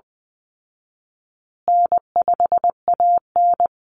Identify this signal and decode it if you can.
Morse code, heard as a keyed tone, about 20 wpm, 700 Hz. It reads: N5AN
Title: Call Signs Repeated 2x - 20wpm